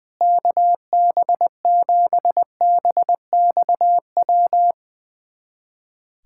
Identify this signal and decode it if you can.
Morse code, heard as a keyed tone, about 20 wpm, 700 Hz. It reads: KB7BXW
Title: USA Callsign DE Callsign 20wpm